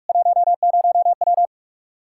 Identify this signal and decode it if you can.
Morse code, heard as a keyed tone, about 46 wpm, 700 Hz. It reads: 10W
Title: Practice QSO - Exchange 2 - Info - 45wpm